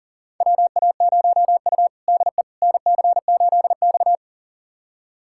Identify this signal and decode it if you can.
Morse code, heard as a keyed tone, about 40 wpm, 700 Hz. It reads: WA0U DE NC8X
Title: USA Callsign DE Callsign - 4 Characters - 40wpm